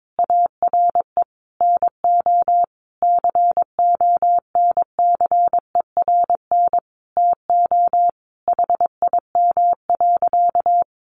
Morse code, heard as a keyed tone, about 22 wpm, 700 Hz. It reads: ARE NO CONCERN TO HIM.
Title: Sentences from Top 1200 Words 22wpm